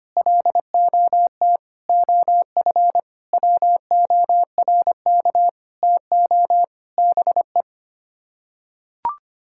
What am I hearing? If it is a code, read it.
Morse code, heard as a keyed tone, about 25 wpm, 700 Hz. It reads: LOT OF WORK TO BE
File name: Sets of 9 Words - 25wpm